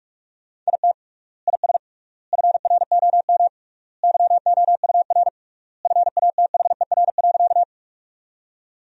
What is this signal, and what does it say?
Morse code, heard as a keyed tone, about 43 wpm, 700 Hz. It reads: IT IS FROM YOUR FATHER.
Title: 3x Word-Spacing - Sentences from Top 300 Words - 45wpm